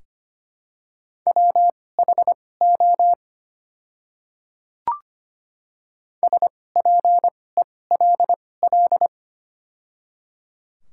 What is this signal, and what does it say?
Morse code, heard as a keyed tone, about 25 wpm, spaced as for 18 wpm, 700 Hz. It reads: WHO SPELL
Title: 2x Character Spacing - Top 200 Words - 25wpm